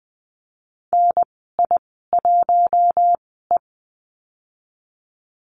Morse code, heard as a keyed tone, about 20 wpm, spaced as for 15 wpm, 700 Hz. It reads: NI1E
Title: 082 Lesson 14g - Number 2 - Call Signs 20wpm